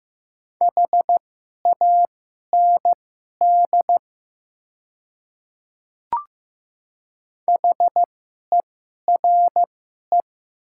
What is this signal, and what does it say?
Morse code, heard as a keyed tone, about 15 wpm, spaced as for 11 wpm, 700 Hz. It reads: HAND HERE
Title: 2x Character Spacing - Top 300 Words - 15wpm